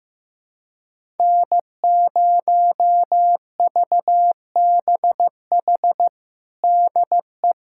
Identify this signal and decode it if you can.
Morse code, heard as a keyed tone, about 14 wpm, 700 Hz. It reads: N0VBH DE
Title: USA Callsign DE Callsign 15wpm